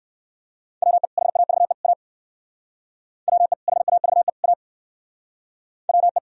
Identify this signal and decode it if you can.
Morse code, heard as a keyed tone, about 57 wpm, 700 Hz. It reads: WE HAVE A WE HAVE A WE